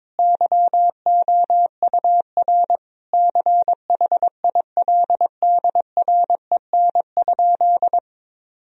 Morse code, heard as a keyed tone, about 22 wpm, 700 Hz. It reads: YOUR CHILDREN?
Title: Sentences from Top 1000 Words - Review Part 2 - 22wpm